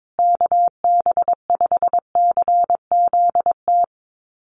Morse code, heard as a keyed tone, about 22 wpm, 700 Hz. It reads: KB5CZT
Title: USA Call Signs Repeated 2x 22wpm